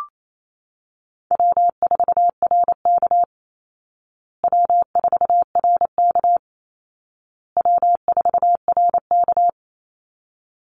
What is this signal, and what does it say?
Morse code, heard as a keyed tone, about 28 wpm, 700 Hz. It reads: W4RK W4RK W4RK
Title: Call Signs Repeated 2x - 28wpm